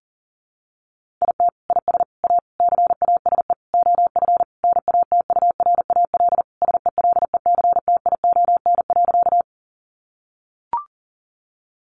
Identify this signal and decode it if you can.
Morse code, heard as a keyed tone, about 40 wpm, 700 Hz. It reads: IT IS A CASE OF NATURAL SELECTION.
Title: Sentences from Top 1400 Words 40wpm